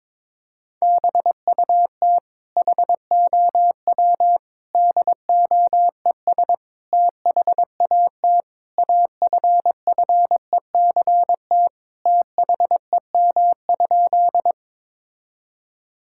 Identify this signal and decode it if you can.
Morse code, heard as a keyed tone, about 22 wpm, 700 Hz. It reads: BUT HOW DOES THAT AFFECT THEM?
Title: Sentences from Top 1000 Words - Review Part 6 - 22wpm